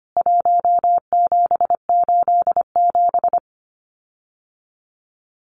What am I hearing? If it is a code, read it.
Morse code, heard as a keyed tone, about 25 wpm, 700 Hz. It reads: 1787
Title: CWT Exchange 25wpm